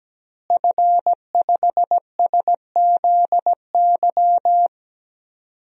Morse code, heard as a keyed tone, about 17 wpm, 700 Hz. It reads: F5SZY
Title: DXCC-100 Call Signs - 5 Characters - Repeated 3x - 17wpm